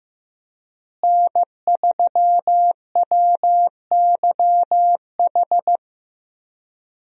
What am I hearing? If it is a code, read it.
Morse code, heard as a keyed tone, about 15 wpm, 700 Hz. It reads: N3WYH
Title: USA Call Signs - 5 Characters - Repeated 3x - 15wpm